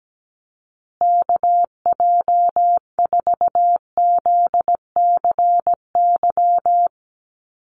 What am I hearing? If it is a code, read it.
Morse code, heard as a keyed tone, about 17 wpm, 700 Hz. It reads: KJ4ZCY
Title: USA Call Signs Repeated 2x 17wpm